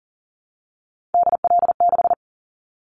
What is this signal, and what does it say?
Morse code, heard as a keyed tone, about 40 wpm, 700 Hz. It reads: DL6
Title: Callsign Prefixes for Top 25 DXCC - Repeated 3x - 40wpm